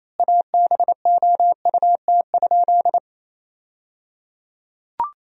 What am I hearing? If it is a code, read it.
Morse code, heard as a keyed tone, about 28 wpm, 700 Hz. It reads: ABOUT?
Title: 3x Word-Spacing - Sentences from Top 100 Words - 28wpm